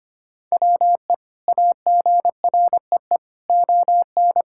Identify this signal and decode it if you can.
Morse code, heard as a keyed tone, about 25 wpm, 700 Hz. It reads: WE AGREE ON
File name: practice / 25wpm / Sentences with 2-Word Phrases - 25wpm